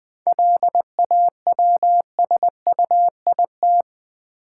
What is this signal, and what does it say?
Morse code, heard as a keyed tone, about 20 wpm, 700 Hz. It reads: LAWSUIT
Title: 7 Letter Words - First 3 Letters Given - 20wpm